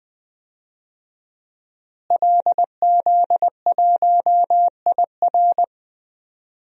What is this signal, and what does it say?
Morse code, heard as a keyed tone, about 20 wpm, 700 Hz. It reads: LZ1IR
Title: DXCC-100 Call Signs - 5 Characters - Repeated 3x - 20wpm